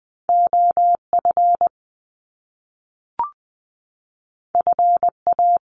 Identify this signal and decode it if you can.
Morse code, heard as a keyed tone, about 20 wpm, 700 Hz. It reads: OF FA
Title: Top Two Letter Combinations in Words 20wpm